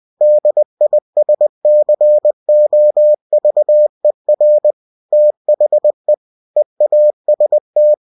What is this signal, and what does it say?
Morse code, heard as a keyed tone, about 20 wpm, 600 Hz. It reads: DISCOVER THE EAST
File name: cw-books / winnie-the-pooh / winnie-20wpm-page-0093